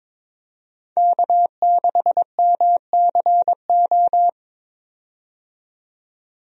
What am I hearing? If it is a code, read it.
Morse code, heard as a keyed tone, about 21 wpm, 700 Hz. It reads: K6MCO
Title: USA Call Signs - 5 Characters - 22wpm